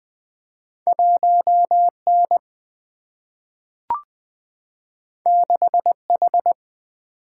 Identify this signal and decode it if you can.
Morse code, heard as a keyed tone, about 20 wpm, 700 Hz. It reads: 1 N 6 H
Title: Single-Letter Single-Number 20wpm